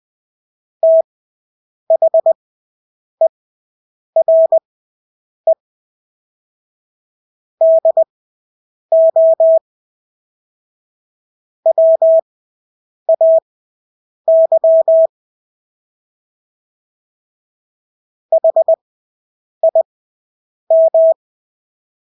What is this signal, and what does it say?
Morse code, heard as a keyed tone, about 20 wpm, spaced as for 8 wpm, 650 Hz. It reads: THERE DO WAY HIM
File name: Engelske ord 20/8 wpm